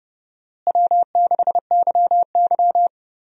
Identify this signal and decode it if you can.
Morse code, heard as a keyed tone, about 31 wpm, 700 Hz. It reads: W6YY
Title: USA Call Signs - 4 Characters - Repeated 2x - 30wpm